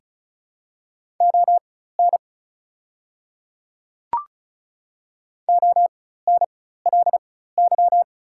Morse code, heard as a keyed tone, about 36 wpm, spaced as for 16 wpm, 700 Hz. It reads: ON ONLY
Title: 4x Character Spacing - Top 300 Words - 35wpm